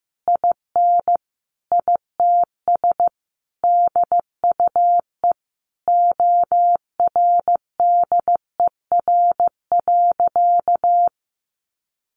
Morse code, heard as a keyed tone, about 15 wpm, 700 Hz. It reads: IN ITS DUE ORDER.
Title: Sentences from Top 900 Words 15wpm